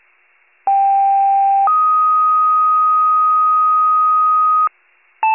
Two-tone page, heard as a one-second tone, then a three-second tone, about 790 then 1250 Hz. It is heard from 0.7 to 4.7 s.
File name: pager / UNKNOWN / wcalert